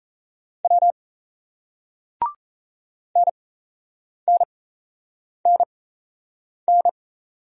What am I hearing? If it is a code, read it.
Morse code, heard as a keyed tone, about 37 wpm, spaced as for 8 wpm, 700 Hz. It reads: W NNNN